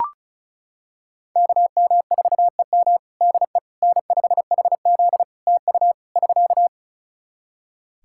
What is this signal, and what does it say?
Morse code, heard as a keyed tone, about 34 wpm, 700 Hz. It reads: KM4EM DE N5HZ TU <SK>
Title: Practice QSO - Exchange 4 - Ending - 35wpm